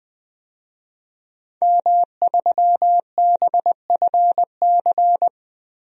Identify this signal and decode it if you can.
Morse code, heard as a keyed tone, about 21 wpm, 700 Hz. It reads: M3BFC
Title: DXCC-100 Call Signs - 5 Characters - 20wpm